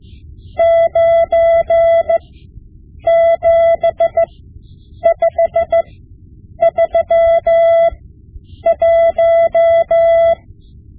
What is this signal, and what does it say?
Morse code, heard as a keyed tone, about 15 wpm, spaced as for 8 wpm, 670 Hz. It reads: 97531
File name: Morse Lessons / Morse Lessons / Tape-5B